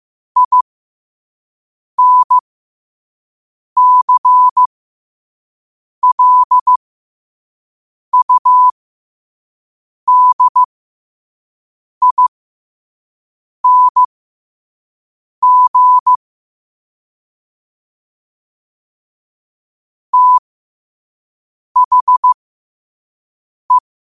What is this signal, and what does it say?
Morse code, heard as a keyed tone, about 15 wpm, spaced as for 5 wpm, 1000 Hz. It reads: INCLUDING THE
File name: qotdfiles / 5wpm